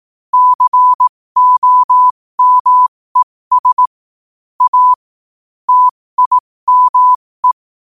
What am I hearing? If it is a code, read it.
Morse code, heard as a keyed tone, about 18 wpm, spaced as for 15 wpm, 1000 Hz. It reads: COMES A TIME
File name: qotdfiles / 15wpm